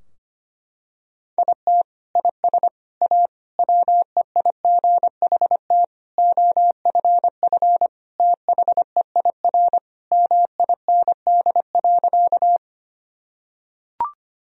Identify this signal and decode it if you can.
Morse code, heard as a keyed tone, about 25 wpm, 700 Hz. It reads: IT IS A WEIGHT OFF THEIR MIND.
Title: Sentences from Top 1000 Words - Review Part 3 - 25wpm